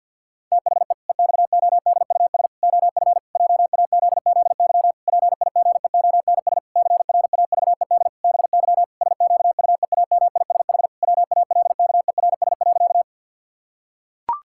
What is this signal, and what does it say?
Morse code, heard as a keyed tone, about 50 wpm, 700 Hz. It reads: THE EXODUS OF JAZZY PIGEONS CRAVED BY SQUEAMISH WALKERS.
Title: Pangram Challenge v2 - 50wpm